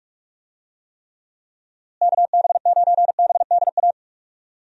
Morse code, heard as a keyed tone, about 46 wpm, 700 Hz. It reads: KB9BDA
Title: USA Call Signs - 6 Characters - Repeated 2x - 45wpm